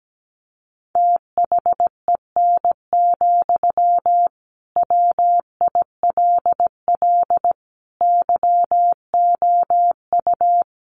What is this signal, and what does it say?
Morse code, heard as a keyed tone, about 17 wpm, 700 Hz. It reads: THEN, WILL YOU
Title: Sentences from Top 1500 Words 17wpm